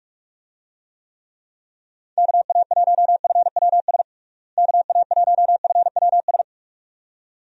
Morse code, heard as a keyed tone, about 43 wpm, 700 Hz. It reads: KA1FWS KA1FWS
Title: USA Call Signs - 6 Characters - Repeated 2x - 45wpm